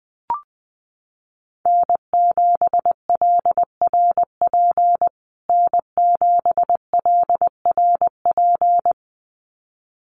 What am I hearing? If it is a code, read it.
Morse code, heard as a keyed tone, about 20 wpm, 700 Hz. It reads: N7LRP N7LRP
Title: USA Call Signs Repeated 2x 20wpm